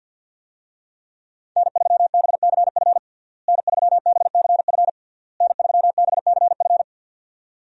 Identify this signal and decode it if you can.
Morse code, heard as a keyed tone, about 51 wpm, 700 Hz. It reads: N3BCF N3BCF N3BCF
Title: USA Call Signs - 5 Characters - Repeated 3x - 50wpm